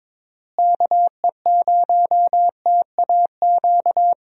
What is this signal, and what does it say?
Morse code, heard as a keyed tone, about 22 wpm, 700 Hz. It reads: KE0TAQ
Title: USA Call Signs - 6 Characters - Repeated 2x - 22wpm